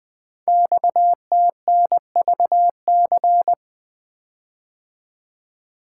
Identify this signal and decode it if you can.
Morse code, heard as a keyed tone, about 20 wpm, 700 Hz. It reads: XTNVC
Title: Code Groups - 5 Letter:Numbers - 20wpm